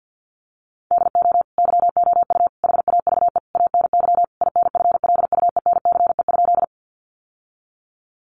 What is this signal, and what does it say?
Morse code, heard as a keyed tone, about 50 wpm, 700 Hz. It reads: DO YOU HAVE ANY INFLUENCE?
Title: Sentences from Top 1100 Words 50wpm